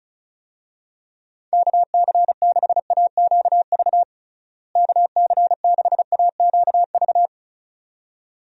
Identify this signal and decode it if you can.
Morse code, heard as a keyed tone, about 34 wpm, 700 Hz. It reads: KC6AQV KC6AQV
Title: USA Call Signs - 6 Characters - Repeated 2x - 35wpm